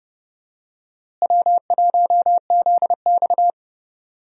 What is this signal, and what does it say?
Morse code, heard as a keyed tone, about 30 wpm, 700 Hz. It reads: W1ZX
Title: USA Call Signs - 4 Characters - Repeated 2x - 30wpm